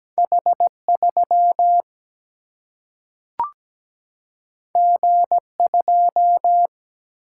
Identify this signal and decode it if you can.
Morse code, heard as a keyed tone, about 17 wpm, 700 Hz. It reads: H3 G2
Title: Single-Letter Single-Number 17wpm